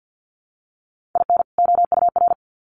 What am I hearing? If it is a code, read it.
Morse code, heard as a keyed tone, about 48 wpm, 700 Hz. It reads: IN OUR
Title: Sets of 2 Words 50wpm